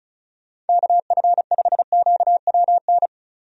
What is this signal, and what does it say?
Morse code, heard as a keyed tone, about 36 wpm, 700 Hz. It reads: KF5QWN
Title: USA Call Signs - 6 Characters - 35wpm